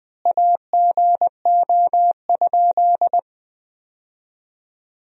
Morse code, heard as a keyed tone, about 20 wpm, 700 Hz. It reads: AGO?
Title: 4x Word-Spacing - Sentences from Top 500 Words - 20wpm